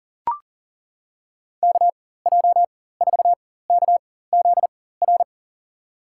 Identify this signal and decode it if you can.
Morse code, heard as a keyed tone, about 40 wpm, spaced as for 19 wpm, 700 Hz. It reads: KJ4KZR
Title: USA Call Signs - 6 Characters - 4x Character Spacing - 40wpm